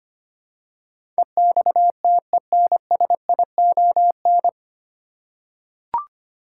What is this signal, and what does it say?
Morse code, heard as a keyed tone, about 25 wpm, 700 Hz. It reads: EXTENSION
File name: Nine Letter Words - 25wpm